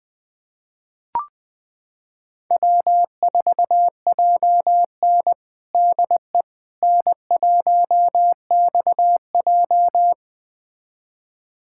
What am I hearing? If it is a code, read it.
Morse code, heard as a keyed tone, about 21 wpm, 700 Hz. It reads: W4JN DE N1XJ
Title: USA Callsign DE Callsign - 4 Characters - 20wpm